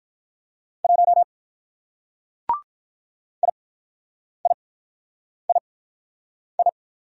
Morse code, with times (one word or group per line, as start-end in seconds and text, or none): 0.8-1.2 1
3.4-6.7 IIII